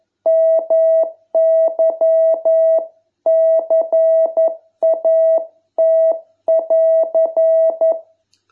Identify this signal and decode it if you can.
Morse code, heard as a keyed tone, about 11 wpm, 660 Hz. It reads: MY CAT<AR>